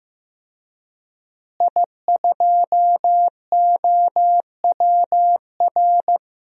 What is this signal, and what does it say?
Morse code, heard as a keyed tone, about 15 wpm, 700 Hz. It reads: I2OWR
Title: DXCC-100 Call Signs - 5 Characters - Repeated 2x - 15wpm